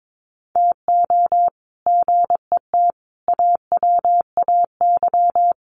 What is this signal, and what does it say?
Morse code, heard as a keyed tone, about 22 wpm, 700 Hz. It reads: TO GET AWAY
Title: Sets of 5 Words Encore2 22wpm